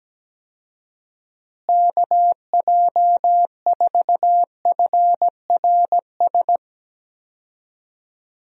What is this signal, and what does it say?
Morse code, heard as a keyed tone, about 17 wpm, 700 Hz. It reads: KJ4FRS
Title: USA Call Signs - 6 Characters - Repeated 3x - 17wpm